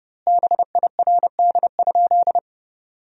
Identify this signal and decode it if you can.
Morse code, heard as a keyed tone, about 32 wpm, 700 Hz. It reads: BIRD?
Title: 4x Word-Spacing - Sentences from Top 400 Words - 30wpm